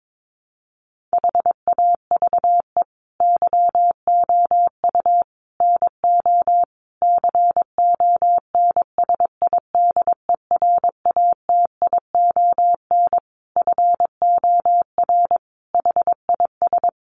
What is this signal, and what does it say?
Morse code, heard as a keyed tone, about 22 wpm, 700 Hz. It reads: HAVE YOU NO CONSIDERATION FOR HIS
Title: Sentences from Top 1500 Words 22wpm